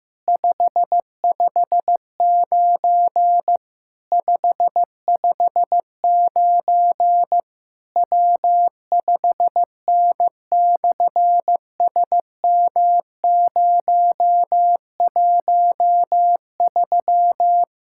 Morse code, heard as a keyed tone, about 15 wpm, 700 Hz. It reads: 559 559 W5N/SM013
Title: SOTA QSO Practice 15wpm